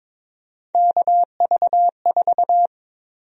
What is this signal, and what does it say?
Morse code, heard as a keyed tone, about 22 wpm, 700 Hz. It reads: KV4
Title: USA Call Sign Prefixes - Repeated 2x - 22wpm